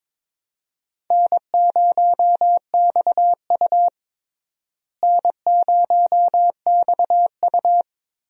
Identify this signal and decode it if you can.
Morse code, heard as a keyed tone, about 22 wpm, 700 Hz. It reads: N0XU N0XU
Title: USA Call Signs - 4 Characters - Repeated 2x - 22wpm